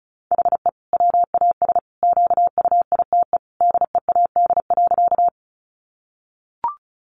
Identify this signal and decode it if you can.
Morse code, heard as a keyed tone, about 34 wpm, 700 Hz. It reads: HE WAS QUITE DEAD.
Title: Sentences from Top 1500 Words 35wpm